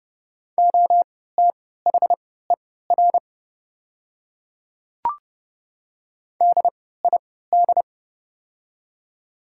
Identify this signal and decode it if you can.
Morse code, heard as a keyed tone, about 30 wpm, spaced as for 17 wpm, 700 Hz. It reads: OTHER DID
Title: 3x Character Spacing - Top 200 Words - 30wpm